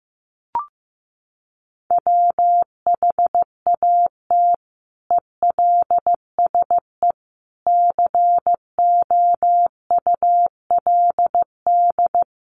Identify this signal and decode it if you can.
Morse code, heard as a keyed tone, about 15 wpm, 700 Hz. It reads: WHAT ELSE COULD
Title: Sentences from Top 800 Words 15wpm